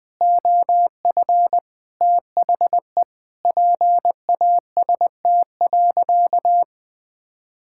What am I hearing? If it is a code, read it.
Morse code, heard as a keyed tone, about 20 wpm, 700 Hz. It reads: OF THE PAST.
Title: Sentences with 6-Word Phrases - 20wpm